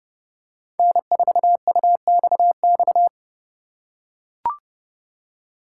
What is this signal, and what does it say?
Morse code, heard as a keyed tone, about 31 wpm, 700 Hz. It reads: N4UXX
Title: USA Call Signs - 5 Characters - 4x Character Spacing - 30wpm